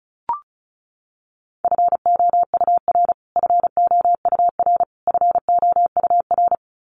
Morse code, heard as a keyed tone, about 34 wpm, 700 Hz. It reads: FOUR FOUR FOUR
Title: Top 500 Words Repeated 35wpm